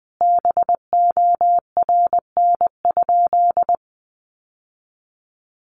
Morse code, heard as a keyed tone, about 20 wpm, 700 Hz. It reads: BORN?
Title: Sentences from Top 1000 Words 20wpm